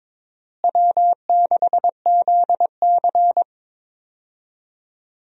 Morse code, heard as a keyed tone, about 22 wpm, 700 Hz. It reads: W6ZC
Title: USA Call Signs - 4 Characters - Repeated 2x - 22wpm